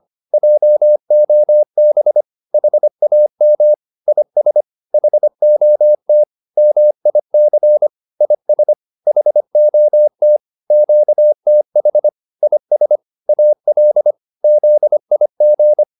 Morse code, heard as a keyed tone, about 25 wpm, 600 Hz. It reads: JOB HAM IS HOT MIC IS HOT QTH IS AL ZIG
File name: PR204a 25